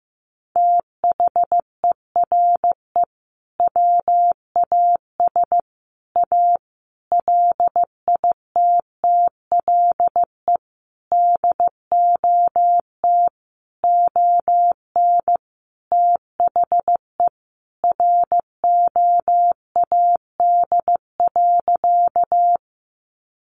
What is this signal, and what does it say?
Morse code, heard as a keyed tone, about 15 wpm, 700 Hz. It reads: THERE WAS A LITTLE DOT ON THE ROAD.
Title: Sentences from Top 1500 Words 15wpm